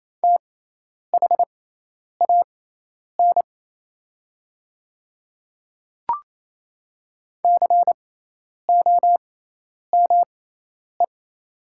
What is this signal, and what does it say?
Morse code, heard as a keyed tone, about 28 wpm, spaced as for 10 wpm, 700 Hz. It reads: THAN COME
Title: 6x Character Spacing - Top 100 Words - 28wpm